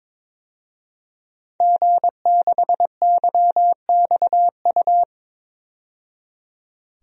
Morse code, heard as a keyed tone, about 22 wpm, 700 Hz. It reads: G6YXU
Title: DXCC-100 Call Signs - 5 Characters - 22wpm